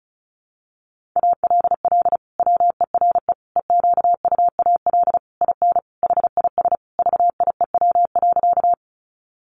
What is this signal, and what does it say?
Morse code, heard as a keyed tone, about 34 wpm, 700 Hz. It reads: ALL WERE EQUAL IN HIS VIEW.
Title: Sentences from Top 1400 Words 35wpm